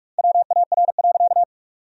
Morse code, heard as a keyed tone, about 46 wpm, 700 Hz. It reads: WAR.
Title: 2x Word-Spacing - Sentences from Top 400 Words - 45wpm